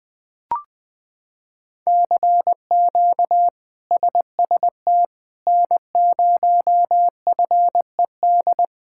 Morse code, heard as a keyed tone, about 20 wpm, 700 Hz. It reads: CQ SST N0FED
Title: K1USN SST QSO Practice - 20wpm